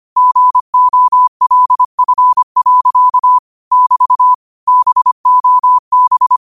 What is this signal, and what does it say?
Morse code, heard as a keyed tone, about 25 wpm, 1000 Hz. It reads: GOLF. <BT> BOB